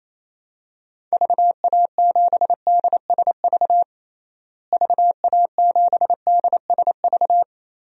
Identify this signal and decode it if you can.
Morse code, heard as a keyed tone, about 28 wpm, 700 Hz. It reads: VA7DSV VA7DSV
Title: DXCC-100 Call Signs - 6 Characters - Repeated 3x - 28wpm